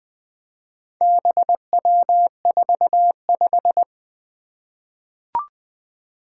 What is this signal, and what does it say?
Morse code, heard as a keyed tone, about 20 wpm, 700 Hz. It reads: BW45
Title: Four Letter-Number Combinations - 20wpm